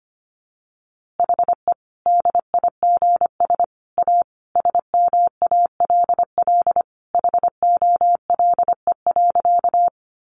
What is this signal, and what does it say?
Morse code, heard as a keyed tone, about 25 wpm, 700 Hz. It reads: HE DIGS A SMALL HOLE.
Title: Easy Phrases 25wpm